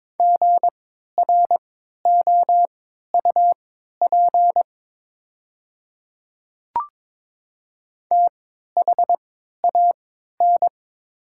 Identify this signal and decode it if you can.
Morse code, heard as a keyed tone, about 22 wpm, spaced as for 12 wpm, 700 Hz. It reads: GROUP THAN
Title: 3x Character Spacing - Top 500 Words - 22wpm